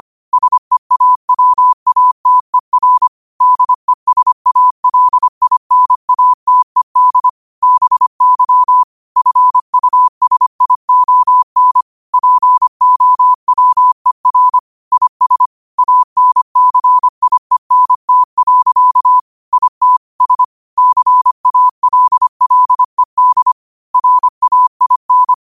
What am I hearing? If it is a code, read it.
Morse code, heard as a keyed tone, about 25 wpm, 1000 Hz. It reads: SEAWATER DESALINATED BY FUSION POWER IS ANCIENT. IT S CALLED RAIN
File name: qotdfiles / 25wpm